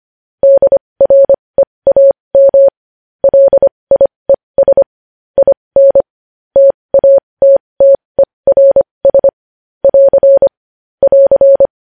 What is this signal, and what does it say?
Morse code, heard as a keyed tone, about 25 wpm, spaced as for 20 wpm, 550 Hz. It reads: DREAM LIES IN TATTERS <AR> <AR>
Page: news-headlines-in-morse-code-at-20-wpm